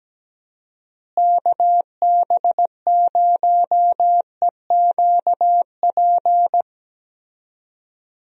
Morse code, heard as a keyed tone, about 17 wpm, 700 Hz. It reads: KB0EQP
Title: USA Call Signs - 6 Characters - Repeated 2x - 17wpm